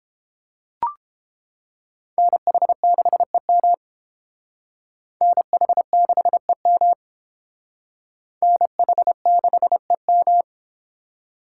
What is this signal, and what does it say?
Morse code, heard as a keyed tone, about 29 wpm, 700 Hz. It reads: NH6EM NH6EM NH6EM